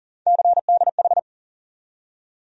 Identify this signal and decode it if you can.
Morse code, heard as a keyed tone, about 40 wpm, 700 Hz. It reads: CDH
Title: USA Call Sign Postfixes - 3 Letters 40wpm